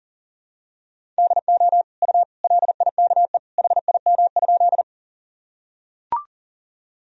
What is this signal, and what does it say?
Morse code, heard as a keyed tone, about 40 wpm, 700 Hz. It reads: DO U LIKE HIM?
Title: Sentences from Top 100 Words and Abbreviations 40wpm